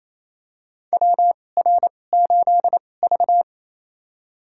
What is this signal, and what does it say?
Morse code, heard as a keyed tone, about 28 wpm, spaced as for 20 wpm, 700 Hz. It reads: WR8V
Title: USA Call Signs - 4 Characters - 2x Character Spacing - 28wpm